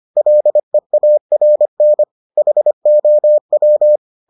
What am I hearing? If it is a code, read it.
Morse code, heard as a keyed tone, about 25 wpm, 600 Hz. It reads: LEARN HOW